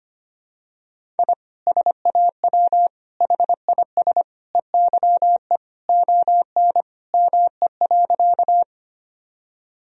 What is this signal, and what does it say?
Morse code, heard as a keyed tone, about 25 wpm, 700 Hz. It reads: I SAW HIS EYE ON ME.
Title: Sentences from Top 1000 Words - Review Part 1 - 25wpm